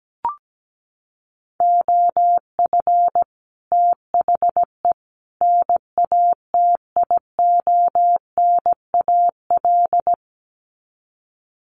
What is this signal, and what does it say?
Morse code, heard as a keyed tone, about 17 wpm, 700 Hz. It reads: OF THE NATIONAL
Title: Sets of 3 Words 17wpm